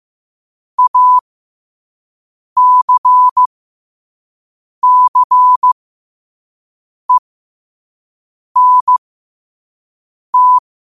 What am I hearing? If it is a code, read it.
Morse code, heard as a keyed tone, about 15 wpm, spaced as for 5 wpm, 1000 Hz. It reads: ACCENT
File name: qotdfiles / 5wpm